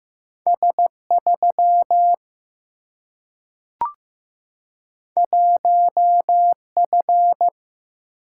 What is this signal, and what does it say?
Morse code, heard as a keyed tone, about 15 wpm, 700 Hz. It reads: S 3 1 F
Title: Single-Letter Single-Number 15wpm